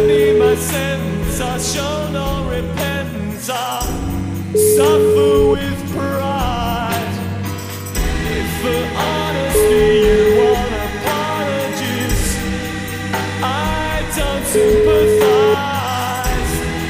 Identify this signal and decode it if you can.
Ringback tone (Europe, 425 Hz), with noise.